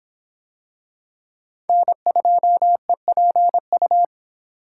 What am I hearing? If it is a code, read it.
Morse code, heard as a keyed tone, about 26 wpm, 700 Hz. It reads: N2EPU